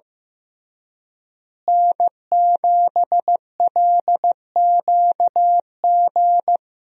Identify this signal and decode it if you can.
Morse code, heard as a keyed tone, about 15 wpm, 700 Hz. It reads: N7LQG
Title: USA Call Signs - 5 Characters - Repeated 3x - 15wpm